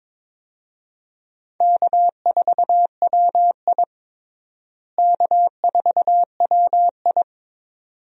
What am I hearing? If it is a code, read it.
Morse code, heard as a keyed tone, about 22 wpm, 700 Hz. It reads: K4WI K4WI
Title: USA Call Signs - 4 Characters - Repeated 2x - 22wpm